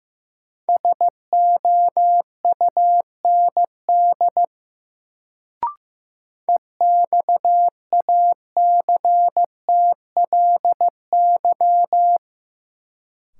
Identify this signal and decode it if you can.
Morse code, heard as a keyed tone, about 15 wpm, 700 Hz. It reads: SOUND EXACTLY
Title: Sentences from Top 900 Words - New Word Drill - 15wpm